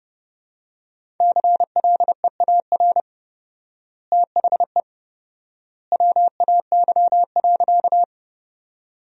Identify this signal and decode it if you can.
Morse code, heard as a keyed tone, about 32 wpm, 700 Hz. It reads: CLEAR THE WAY.
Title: 4x Word-Spacing - Sentences from Top 500 Words - 30wpm